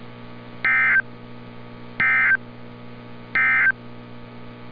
Emergency Alert System (SAME) tones, heard as harsh buzzing bursts. It is heard from 0.5 to 3.8 s.
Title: NOAA Weather Radio - Altoona, PA